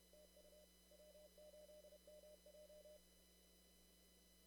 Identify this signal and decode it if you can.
Morse code, heard as a keyed tone, about 30 wpm, 600 Hz, with noise. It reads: TU W8MJ
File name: CW / kt5j wpx cw-op-k5pi